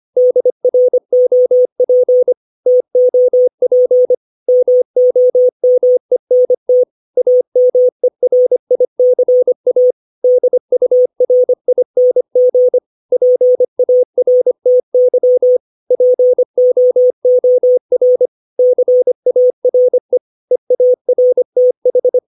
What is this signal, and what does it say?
Morse code, heard as a keyed tone, about 25 wpm, 500 Hz. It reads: DROP TOP MOMENT AMERICA DURING PARTY POOR CARE EARTH
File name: 632 words / 632 Words 25WPM